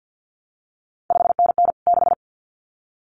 Morse code, heard as a keyed tone, about 50 wpm, 700 Hz. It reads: 5NN 6
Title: CQ WW DX Contest Exchange - 50wpm